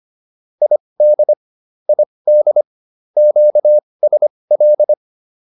Morse code, heard as a keyed tone, about 25 wpm, spaced as for 20 wpm, 590 Hz. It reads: ID ID QSL